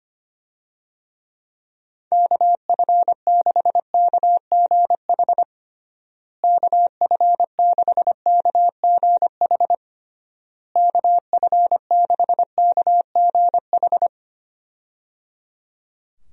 Morse code, heard as a keyed tone, about 25 wpm, 700 Hz. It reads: KF6KGH KF6KGH KF6KGH
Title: USA Call Signs - 6 Characters - Repeated 3x - 25wpm